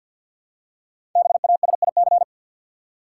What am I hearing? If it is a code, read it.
Morse code, heard as a keyed tone, about 48 wpm, 700 Hz. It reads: BASIC
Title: Five Letter Words - 50wpm